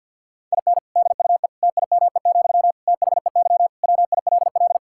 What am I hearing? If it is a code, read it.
Morse code, heard as a keyed tone, about 50 wpm, 700 Hz. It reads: IN DUE TIME, THEY WILL
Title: Sentences from Top 1000 Words - Review Part 6 - 50wpm